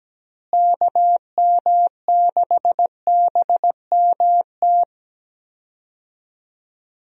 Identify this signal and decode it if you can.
Morse code, heard as a keyed tone, about 17 wpm, 700 Hz. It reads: KM6BMT
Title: USA Call Signs - 6 Characters - Repeated 3x - 17wpm